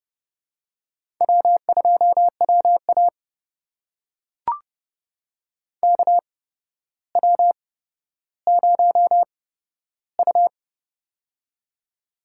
Morse code, heard as a keyed tone, about 31 wpm, spaced as for 13 wpm, 700 Hz. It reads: W2WA KW0U
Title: USA Call Signs - 4 Characters - 8x Character Spacing - 30wpm